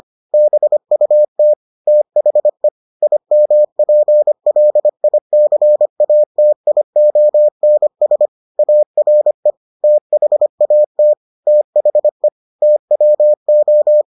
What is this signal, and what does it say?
Morse code, heard as a keyed tone, about 25 wpm, 600 Hz. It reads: BUT THE IMPLICATIONS ARE THAT THE TWO